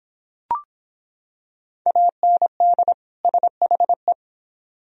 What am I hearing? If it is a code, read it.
Morse code, heard as a keyed tone, about 26 wpm, 700 Hz. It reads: AND SHE